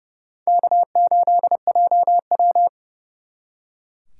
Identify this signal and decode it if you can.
Morse code, heard as a keyed tone, about 39 wpm, 700 Hz. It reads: K8JW